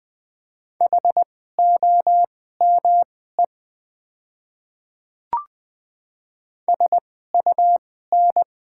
Morse code, heard as a keyed tone, about 20 wpm, spaced as for 14 wpm, 700 Hz. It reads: HOME SUN
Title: 2x Character Spacing - Top 300 Words - 20wpm